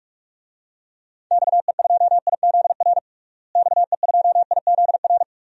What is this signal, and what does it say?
Morse code, heard as a keyed tone, about 44 wpm, 700 Hz. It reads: XE2IZR XE2IZR
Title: DXCC-100 Call Signs - 6 Characters - Repeated 2x - 45wpm